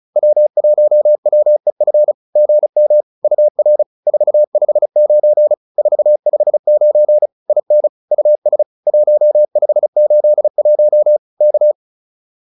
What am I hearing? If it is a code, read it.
Morse code, heard as a keyed tone, about 34 wpm, 590 Hz. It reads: W1WEF GM UR 459 459 IN US 1581 K
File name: POTA306 35